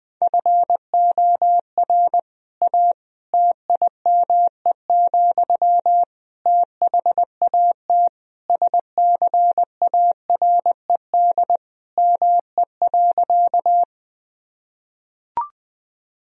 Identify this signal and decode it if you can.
Morse code, heard as a keyed tone, about 20 wpm, 700 Hz. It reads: FOR A TIME, THAT SCARED ME.
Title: Sentences from Top 1000 Words - Review Part 4 - 20wpm